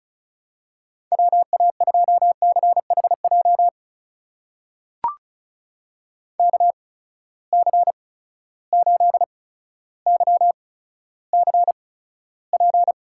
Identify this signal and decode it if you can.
Morse code, heard as a keyed tone, about 34 wpm, spaced as for 15 wpm, 700 Hz. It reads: WA2CHJ KC8YCP
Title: USA Call Signs - 6 Characters - 8x Character Spacing - 35wpm